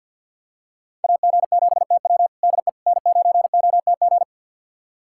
Morse code, heard as a keyed tone, about 45 wpm, 700 Hz. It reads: AG7TW DE N9OTG
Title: USA Callsign DE Callsign - 5 Characters - 50wpm